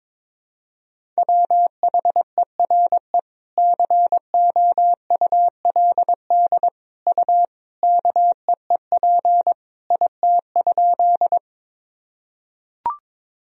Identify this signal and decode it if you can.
Morse code, heard as a keyed tone, about 22 wpm, 700 Hz. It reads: WHERE COULD U KEEP IT?
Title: Sentences from Top 300 Words and Abbreviations 22wpm